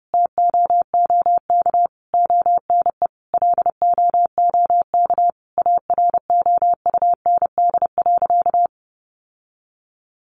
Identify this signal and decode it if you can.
Morse code, heard as a keyed tone, about 30 wpm, 700 Hz. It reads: TOOK ONE LOOK AROUND.
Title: Sentences from Top 600 Words 30wpm